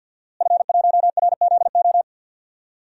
Morse code, heard as a keyed tone, about 49 wpm, 700 Hz. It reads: F1RZO
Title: DXCC-100 Call Signs - 5 Characters - Repeated 2x - 50wpm